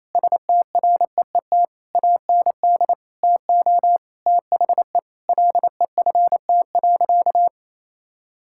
Morse code, heard as a keyed tone, about 28 wpm, 700 Hz. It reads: STREET AND TO THE LEFT.
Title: Sentences from Top 1600 Words 28wpm